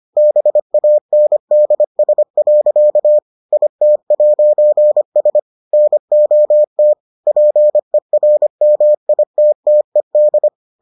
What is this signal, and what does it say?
Morse code, heard as a keyed tone, about 25 wpm, 600 Hz. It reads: BANDS. IT'S NOT PERMITTED